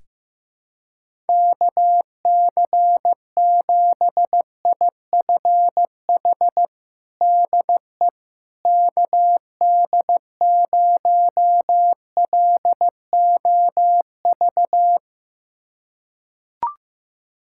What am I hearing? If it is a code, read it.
Morse code, heard as a keyed tone, about 15 wpm, 700 Hz. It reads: KC7IFH DE KD0LOV
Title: USA Callsign DE Callsign 15wpm